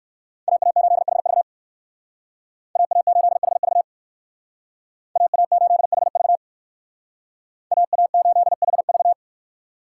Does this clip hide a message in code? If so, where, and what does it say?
Yes: Morse code from 0.5 to 9.1 s, heard as a keyed tone, about 59 wpm, 700 Hz. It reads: AA8HV AA8HV AA8HV AA8HV